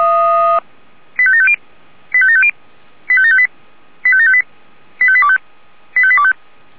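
Five-tone selective call (ZVEI), heard as quick runs of five tones, tones of about 70 ms. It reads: 85750 85750 85758 85758 85825 85825
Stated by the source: ZVEI five-tone call received with an SDR.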